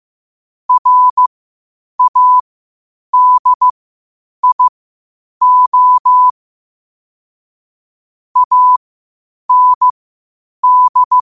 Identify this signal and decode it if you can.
Morse code, heard as a keyed tone, about 15 wpm, spaced as for 8 wpm, 1000 Hz. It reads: RADIO AND
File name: qotdfiles / 8wpm